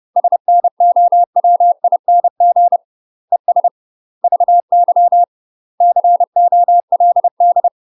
Morse code, heard as a keyed tone, about 30 wpm, 700 Hz. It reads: SNOWING ES VY COLD
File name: UK MASTER QSOs TRAINING / 30WPM / UK MASTER QSOs 0036